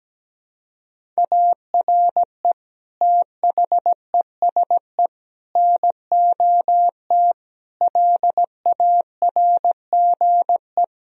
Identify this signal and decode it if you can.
Morse code, heard as a keyed tone, about 17 wpm, 700 Hz. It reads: ARE THESE NOT LARGE
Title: Sentences from Top 1000 Words - Review Part 1 - 17wpm